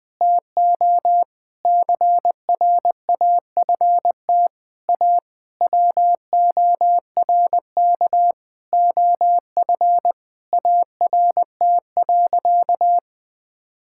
Morse code, heard as a keyed tone, about 20 wpm, 700 Hz. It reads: TO CRAFT A WORK OF ART.
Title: Sentences from Top 1000 Words - Review Part 6 - 20wpm